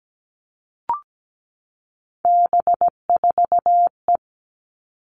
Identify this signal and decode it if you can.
Morse code, heard as a keyed tone, about 17 wpm, 700 Hz. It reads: B4E
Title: All Cut Numbers - 3-Digit 17wpm